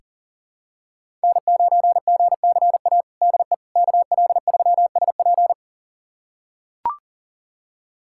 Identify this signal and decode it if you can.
Morse code, heard as a keyed tone, about 40 wpm, 700 Hz. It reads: N9GCA DE KL3SP
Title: USA Callsign DE Callsign - 5 Characters - 40wpm